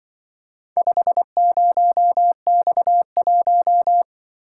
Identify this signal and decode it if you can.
Morse code, heard as a keyed tone, about 25 wpm, 700 Hz. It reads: 50X1
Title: Speed-Racing - Four Letter-Number Combinations - 20wpm